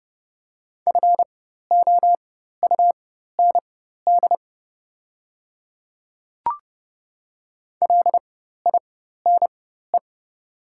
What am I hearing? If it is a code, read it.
Morse code, heard as a keyed tone, about 31 wpm, spaced as for 14 wpm, 700 Hz. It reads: FOUND LINE
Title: 4x Character Spacing - Top 300 Words - 30wpm